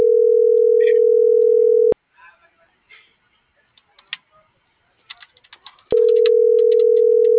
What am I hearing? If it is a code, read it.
Ringback tone (North America), 440 + 480 Hz.